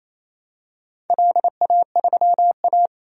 Morse code, heard as a keyed tone, about 28 wpm, 700 Hz. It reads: LA3A
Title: DXCC-100 Call Signs - 4 Characters - 28wpm